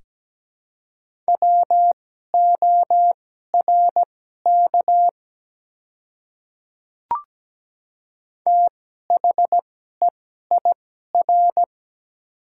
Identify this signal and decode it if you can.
Morse code, heard as a keyed tone, about 17 wpm, spaced as for 12 wpm, 700 Hz. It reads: WORK THEIR
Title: 2x Character Spacing - Top 200 Words - 17wpm